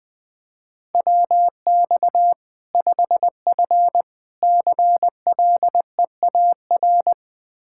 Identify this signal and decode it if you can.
Morse code, heard as a keyed tone, about 21 wpm, 700 Hz. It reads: WX 5F CLEAR
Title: Practice QSO - Exchange 2 - Info - 20wpm